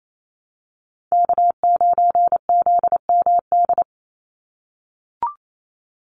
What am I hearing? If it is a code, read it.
Morse code, heard as a keyed tone, about 28 wpm, 700 Hz. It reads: K9ZMD
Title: NA SOTA Chaser Callsigns 28wpm